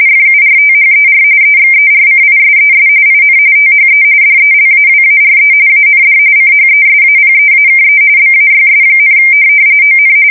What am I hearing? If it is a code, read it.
RTTY, about 45.45 baud. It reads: FURTHER EXERCISES IN THE DELAWARE AND CHESAPEAKE BAYS, ON
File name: fsk170 / USS-New-Jersey